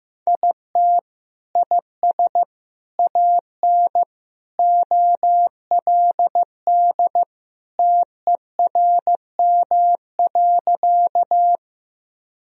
Morse code, heard as a keyed tone, about 15 wpm, 700 Hz. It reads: IT IS AN OLD TERM.
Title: Sentences from Top 1000 Words - Review Part 4 - 15wpm